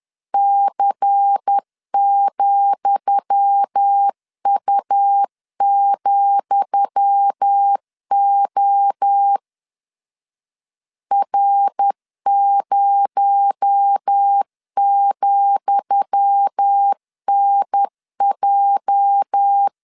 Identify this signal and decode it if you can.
Morse code, heard as a keyed tone, about 10 wpm, 800 Hz. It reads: C,U,O R0,NJ
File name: LEZIONI / 10 SIGNS / 10D COMA 10wpm 3013